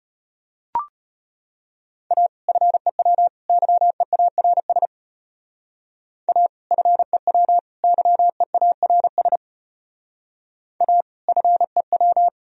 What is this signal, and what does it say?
Morse code, heard as a keyed tone, about 33 wpm, 700 Hz. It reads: A FEW YEARS A FEW YEARS A FEW